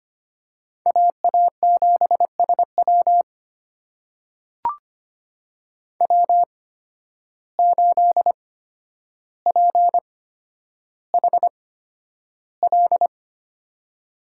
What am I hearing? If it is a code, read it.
Morse code, heard as a keyed tone, about 25 wpm, spaced as for 11 wpm, 700 Hz. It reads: AA7SW W8PHL
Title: USA Call Signs - 5 Characters - 8x Character Spacing - 25wpm